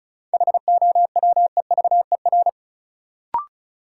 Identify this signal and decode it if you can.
Morse code, heard as a keyed tone, about 36 wpm, 700 Hz. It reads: HOWEVER
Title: Sentences from Top 600 Words - New Word Drill - 35wpm